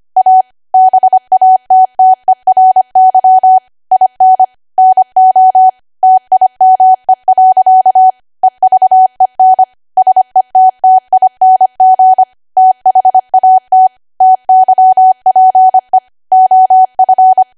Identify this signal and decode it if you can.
Morse code, heard as a keyed tone, about 25 wpm, 750 Hz. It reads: A BATTERY IN NO TIME. EVEN SETTING THAT TYPE OF
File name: Archive / 25 WPM / 25WPM